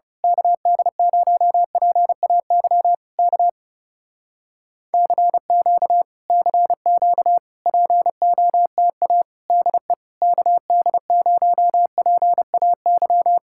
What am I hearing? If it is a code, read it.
Morse code, heard as a keyed tone, about 30 wpm, 700 Hz. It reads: KD0PAY K CQ CQ POTA DE KD0PAY
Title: Speed-Racing Common QSO Phrases - 30wpm